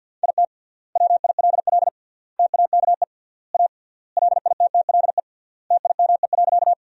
Morse code, heard as a keyed tone, about 48 wpm, 700 Hz. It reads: IT WILL TAKE A LITTLE TIME.
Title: 3x Word-Spacing - Sentences from Top 200 Words - 50wpm